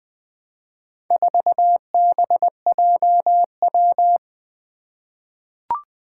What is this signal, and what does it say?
Morse code, heard as a keyed tone, about 20 wpm, 700 Hz. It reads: 4BJW
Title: Four Challenging Letter-Number Combinations 20wpm